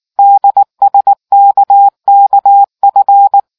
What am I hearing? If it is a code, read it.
Morse code, heard as a keyed tone, about 18 wpm, 800 Hz. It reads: DSKKF